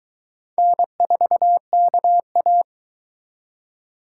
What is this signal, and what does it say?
Morse code, heard as a keyed tone, about 21 wpm, 700 Hz. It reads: N4KA
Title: Speed Racing - USA Call Signs 17wpm